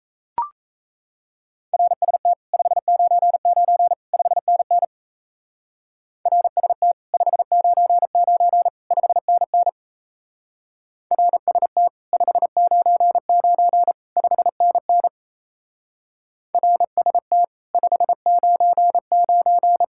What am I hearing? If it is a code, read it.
Morse code, heard as a keyed tone, about 32 wpm, 700 Hz. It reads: RST 599 5NN RST 599 5NN RST 599 5NN RST 599
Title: Speed-Racing Common QSO Phrases - 28wpm